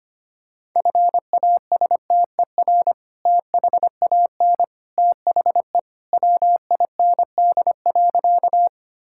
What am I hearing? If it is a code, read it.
Morse code, heard as a keyed tone, about 25 wpm, 700 Hz. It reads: FASTER THAN THE WIND.
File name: practice / 25wpm / Sentences with 2-Word Phrases - 25wpm